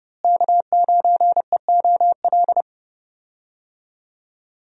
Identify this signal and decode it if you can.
Morse code, heard as a keyed tone, about 30 wpm, 700 Hz. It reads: K9EOL
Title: USA Call Signs - 5 Characters - Repeated 2x - 30wpm